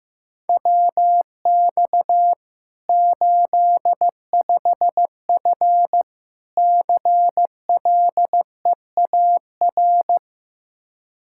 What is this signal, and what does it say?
Morse code, heard as a keyed tone, about 15 wpm, 700 Hz. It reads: WX 85F CLEAR
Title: Practice QSO - Exchange 2 - Info - 15wpm